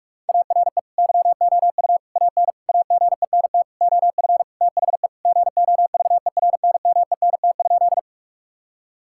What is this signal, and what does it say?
Morse code, heard as a keyed tone, about 46 wpm, 700 Hz. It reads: ARE YOU AN AGENT OF THE GOVERNMENT?
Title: Sentences from Top 1600 Words 45wpm